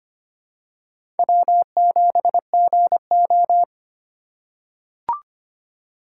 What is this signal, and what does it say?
Morse code, heard as a keyed tone, about 25 wpm, 700 Hz. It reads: W7GO
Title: USA Call Signs - 6x Character Spacing - 25wpm